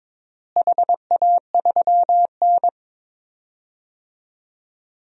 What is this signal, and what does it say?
Morse code, heard as a keyed tone, about 21 wpm, 700 Hz. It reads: HA3N
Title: DXCC-100 Call Signs - 4 Characters - Repeated 3x - 22wpm